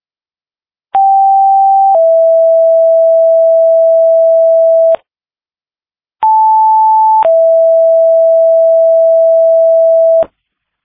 Two-tone page, heard as a one-second tone, then a three-second tone, about 780 then 660 Hz, about 880 then 660 Hz. It is heard from 0.9 to 10.2 s.
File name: Frederika Fire 2026 03 05 19 09 14